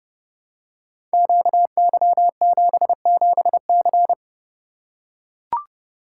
Morse code, heard as a keyed tone, about 30 wpm, 700 Hz. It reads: QY77C
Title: Code Groups - 5 Letter:Numbers - 30wpm